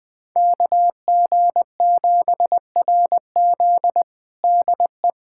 Morse code, heard as a keyed tone, about 20 wpm, 700 Hz. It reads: KG7RZ DE
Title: USA Callsign DE Callsign 20wpm